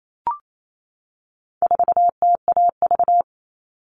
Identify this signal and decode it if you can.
Morse code, heard as a keyed tone, about 28 wpm, 700 Hz. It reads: 4TAV
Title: All Cut Numbers - 4-Digit 28wpm